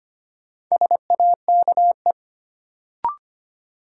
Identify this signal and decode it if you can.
Morse code, heard as a keyed tone, about 25 wpm, 700 Hz. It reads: SAKE
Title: Sets of 4 Words - New Word Drill - 25wpm